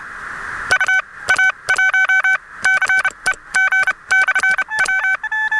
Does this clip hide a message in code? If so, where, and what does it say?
Yes: Morse code from 0.7 to 5.2 s, heard as a keyed tone, about 31 wpm, spaced as for 22 wpm, 1500 Hz, with noise. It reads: UA1CEG/W